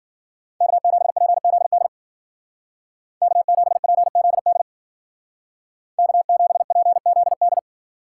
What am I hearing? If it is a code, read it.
Morse code, heard as a keyed tone, about 49 wpm, 700 Hz. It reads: K7PZD K7PZD K7PZD
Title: Speed Racing - USA 1x3 Call Signs 40wpm